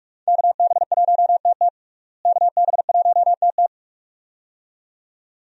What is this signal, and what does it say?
Morse code, heard as a keyed tone, about 46 wpm, 700 Hz. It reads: KB1TT KB1TT
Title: USA Call Signs - 5 Characters - Repeated 2x - 45wpm